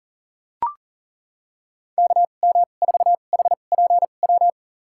Morse code, heard as a keyed tone, about 40 wpm, spaced as for 29 wpm, 700 Hz. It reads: KM4HPW
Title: USA Call Signs - 6 Characters - 2x Character Spacing - 40wpm